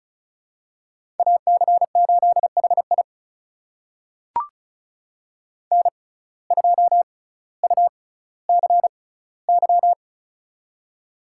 Morse code, heard as a keyed tone, about 34 wpm, spaced as for 18 wpm, 700 Hz. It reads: AC8HI N2UCY
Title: USA Call Signs - 5 Characters - 6x Character Spacing - 35wpm